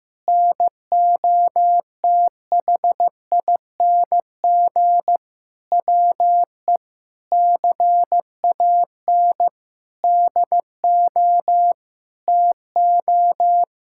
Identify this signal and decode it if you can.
Morse code, heard as a keyed tone, about 14 wpm, 700 Hz. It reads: NOTHING WE CAN DO TO
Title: Sentences from Top 1000 Words - Review Part 6 - 15wpm